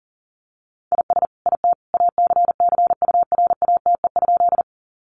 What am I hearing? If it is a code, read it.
Morse code, heard as a keyed tone, about 40 wpm, 700 Hz. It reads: IS IT ACCURATE?
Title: Sentences from Top 1200 Words 40wpm